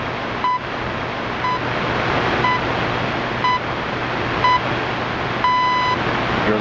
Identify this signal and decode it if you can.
Radio time signal (time pips), with speech. It is heard from 0.4 to 5.7 s.